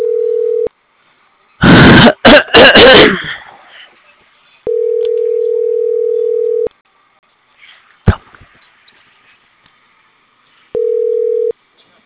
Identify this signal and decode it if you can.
Ringback tone (North America), 440 + 480 Hz.